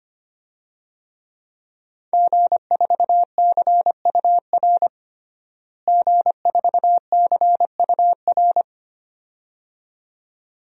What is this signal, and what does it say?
Morse code, heard as a keyed tone, about 25 wpm, 700 Hz. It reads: G4CUR G4CUR
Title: DXCC-100 Call Signs - 5 Characters - Repeated 2x - 25wpm